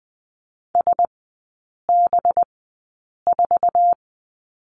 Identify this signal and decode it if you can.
Morse code, heard as a keyed tone, about 20 wpm, spaced as for 8 wpm, 690 Hz. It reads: SB4
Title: 212 Lesson 34e - Number 8 - Three Characters 20wpm